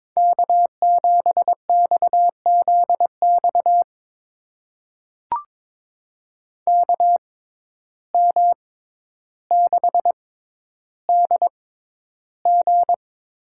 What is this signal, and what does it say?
Morse code, heard as a keyed tone, about 22 wpm, spaced as for 11 wpm, 700 Hz. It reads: K7XZX KM6DG
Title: USA Call Signs - 6x Character Spacing - 22wpm